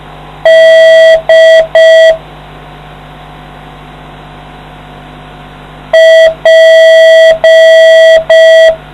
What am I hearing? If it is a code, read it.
Morse code, heard as a keyed tone, about 5 wpm, 650 Hz. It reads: D P